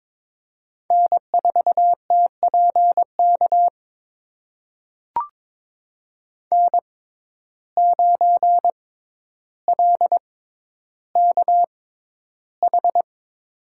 Morse code, heard as a keyed tone, about 22 wpm, spaced as for 11 wpm, 700 Hz. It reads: N4TPK N9LKH
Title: USA Call Signs - 5 Characters - 6x Character Spacing - 22wpm